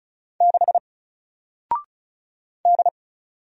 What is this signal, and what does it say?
Morse code, heard as a keyed tone, about 35 wpm, 700 Hz. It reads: 6 D
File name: Single-Letter-Number-35wpm